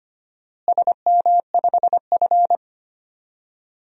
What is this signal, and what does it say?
Morse code, heard as a keyed tone, about 25 wpm, 700 Hz. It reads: SM5F
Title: DXCC-100 Call Signs - 4 Characters - Repeated 3x - 25wpm